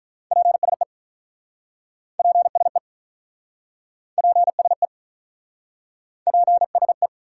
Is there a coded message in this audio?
Yes: Morse code from 0.3 to 7.1 s, heard as a keyed tone, about 47 wpm, 700 Hz. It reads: PSE PSE PSE PSE